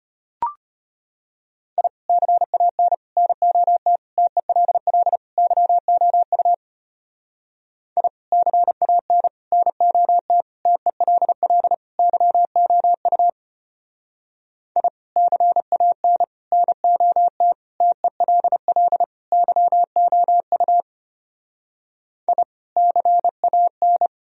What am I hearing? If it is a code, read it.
Morse code, heard as a keyed tone, about 33 wpm, 700 Hz. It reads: I CAN NOT TELL YOU I CAN NOT TELL YOU I CAN NOT TELL YOU I CAN